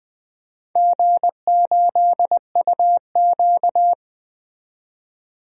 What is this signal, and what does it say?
Morse code, heard as a keyed tone, about 21 wpm, 700 Hz. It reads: G8UQ
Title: DXCC-100 Call Signs - 4 Characters - 20wpm